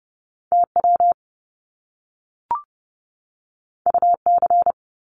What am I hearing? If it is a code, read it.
Morse code, heard as a keyed tone, about 30 wpm, 700 Hz. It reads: TW UC
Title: Top Two Letter Combinations in Words 30wpm